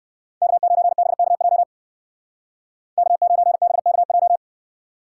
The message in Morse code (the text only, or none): K9BKJ K9BKJ